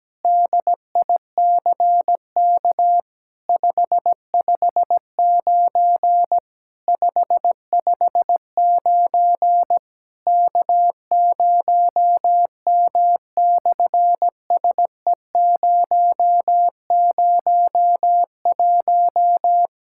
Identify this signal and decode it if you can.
Morse code, heard as a keyed tone, about 17 wpm, 700 Hz. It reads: DICK 559 559 K0M/SE001
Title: SOTA QSO Practice 17wpm